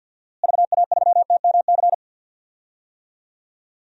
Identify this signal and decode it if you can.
Morse code, heard as a keyed tone, about 47 wpm, 700 Hz. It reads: VA3TMC
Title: DXCC-100 Call Signs - 6 Characters - Repeated 2x - 50wpm